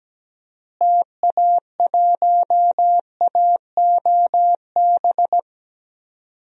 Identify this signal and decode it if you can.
Morse code, heard as a keyed tone, about 17 wpm, 700 Hz. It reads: TA1AOB
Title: DXCC-100 Call Signs - 6 Characters - Repeated 2x - 17wpm